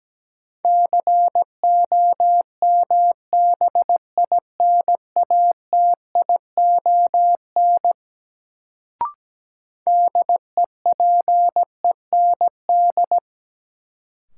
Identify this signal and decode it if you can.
Morse code, heard as a keyed tone, about 17 wpm, 700 Hz. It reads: COMBINATION DEPEND
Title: Sentences from Top 1300 Words - New Word Drill Alt - 17wpm